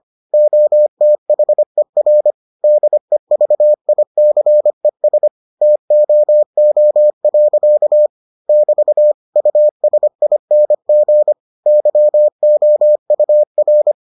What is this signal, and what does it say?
Morse code, heard as a keyed tone, about 25 wpm, 600 Hz. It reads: OTHER DEVICES TOO. <BT> USING YOUR